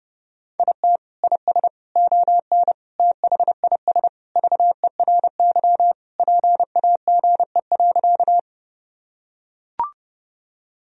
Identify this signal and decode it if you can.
Morse code, heard as a keyed tone, about 30 wpm, 700 Hz. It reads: IT IS ON THIS VERY PAGE.
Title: Sentences from Top 1000 Words - Review Part 1 - 30wpm